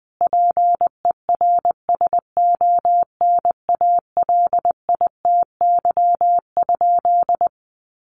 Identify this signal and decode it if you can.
Morse code, heard as a keyed tone, about 20 wpm, 700 Hz. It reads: PERSONALITY?
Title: Sentences from Top 1400 Words 20wpm